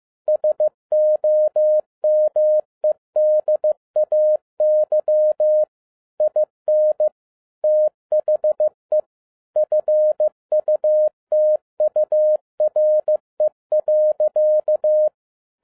Morse code, heard as a keyed tone, about 15 wpm, 610 Hz. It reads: SOMEDAY IN THE FUTURE.